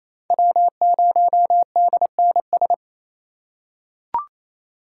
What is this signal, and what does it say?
Morse code, heard as a keyed tone, about 28 wpm, 700 Hz. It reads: W0DNS
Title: USA Call Signs - 5 Characters - 2x Character Spacing - 28wpm